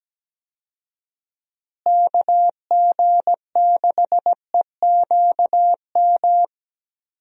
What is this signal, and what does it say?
Morse code, heard as a keyed tone, about 17 wpm, 700 Hz. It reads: KG6EQM
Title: USA Call Signs - 6 Characters - Repeated 3x - 17wpm